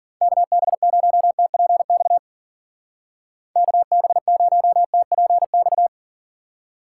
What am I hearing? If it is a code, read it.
Morse code, heard as a keyed tone, about 51 wpm, 700 Hz. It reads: KB0TPX KB0TPX